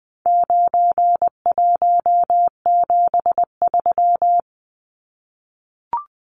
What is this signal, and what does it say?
Morse code, heard as a keyed tone, about 20 wpm, 700 Hz. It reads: 9173
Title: Numbers - 4-Digit 20wpm